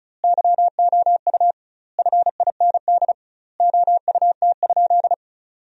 Morse code, heard as a keyed tone, about 34 wpm, 700 Hz. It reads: YOU FIND OUT?
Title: 2x Word-Spacing - Sentences from Top 100 Words - 35wpm